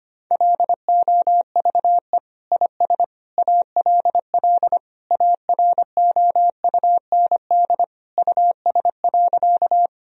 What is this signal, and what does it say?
Morse code, heard as a keyed tone, about 25 wpm, 700 Hz. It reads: LOVE IS ALL AROUND US.
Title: Sentences from Top 1000 Words - Review Part 3 - 25wpm